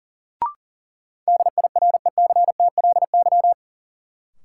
Morse code, heard as a keyed tone, about 40 wpm, 700 Hz. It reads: DIRECTLY
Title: Sentences from Top 800 Words - New Word Drill - 40wpm